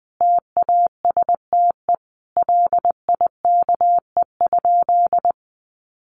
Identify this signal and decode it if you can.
Morse code, heard as a keyed tone, about 20 wpm, 700 Hz. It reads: TASTE LIKE?
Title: Sentences from Top 1100 Words 20wpm